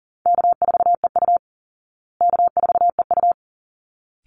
Morse code, heard as a keyed tone, about 40 wpm, 700 Hz. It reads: K4EU K4EU
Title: Easy Call Signs Repeated 2x - 40wpm